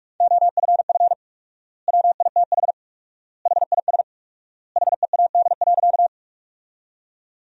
Morse code, heard as a keyed tone, about 46 wpm, 700 Hz. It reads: OFF WITH HIS HEAD.
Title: 4x Word-Spacing - Sentences from Top 300 Words - 45wpm